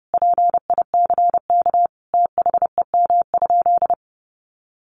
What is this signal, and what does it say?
Morse code, heard as a keyed tone, about 30 wpm, 700 Hz. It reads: PICK THEM?
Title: Sentences from Top 1000 Words 30wpm